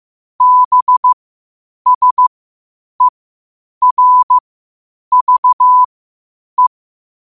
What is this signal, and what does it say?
Morse code, heard as a keyed tone, about 15 wpm, spaced as for 8 wpm, 1000 Hz. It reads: BSERVE
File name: qotdfiles / 8wpm